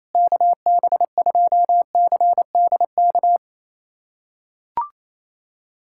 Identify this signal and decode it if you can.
Morse code, heard as a keyed tone, about 28 wpm, 700 Hz. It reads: KB2CDK
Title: USA Callsign DE Callsign 28wpm